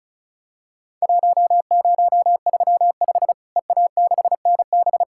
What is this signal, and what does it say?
Morse code, heard as a keyed tone, about 34 wpm, 700 Hz. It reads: 1035 EA6NB
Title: Worked All Europe DX Contest - QTC Exchange - 35wpm